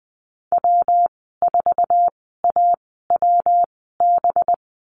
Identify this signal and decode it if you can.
Morse code, heard as a keyed tone, about 20 wpm, spaced as for 14 wpm, 700 Hz. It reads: W4AWB
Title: 231 Lesson 37g - Letter x - Call Signs 20wpm